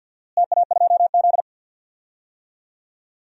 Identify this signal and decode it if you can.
Morse code, heard as a keyed tone, about 50 wpm, 700 Hz. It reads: TA2Z
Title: DXCC-100 Call Signs - 4 Characters - Repeated 3x - 50wpm